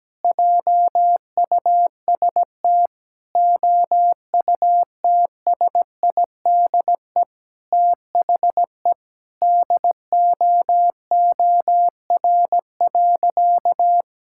Morse code, heard as a keyed tone, about 17 wpm, 700 Hz. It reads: JUST OUTSIDE THE DOOR.
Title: Sentences from Top 1000 Words - Review Part 6 - 17wpm